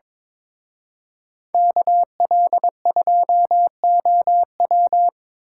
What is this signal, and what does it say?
Morse code, heard as a keyed tone, about 25 wpm, 700 Hz. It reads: KL2OW
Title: Speed Racing - USA Call Signs 22wpm